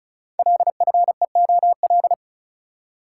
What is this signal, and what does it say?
Morse code, heard as a keyed tone, about 35 wpm, 690 Hz. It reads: LFEOL
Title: Code Groups - 5 Letters - 35wpm